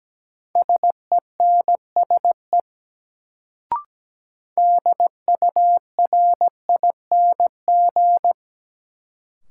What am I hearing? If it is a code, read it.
Morse code, heard as a keyed tone, about 17 wpm, 700 Hz. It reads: SENSE DURING
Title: Sentences from Top 700 Words - New Word Drill - 17wpm